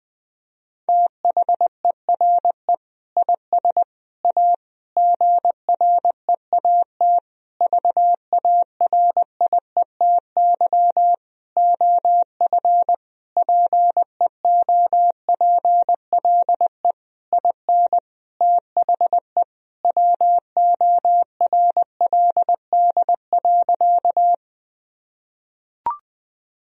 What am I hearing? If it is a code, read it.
Morse code, heard as a keyed tone, about 20 wpm, 700 Hz. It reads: THERE IS A GREAT VARIETY OF PEOPLE IN THE WORLD.
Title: Sentences from Top 1000 Words - Review Part 4 - 20wpm